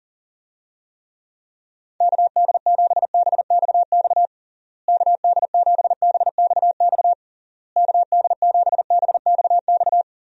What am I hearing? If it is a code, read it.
Morse code, heard as a keyed tone, about 40 wpm, 700 Hz. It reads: KD7BXX KD7BXX KD7BXX
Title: USA Call Signs - 6 Characters - Repeated 3x - 40wpm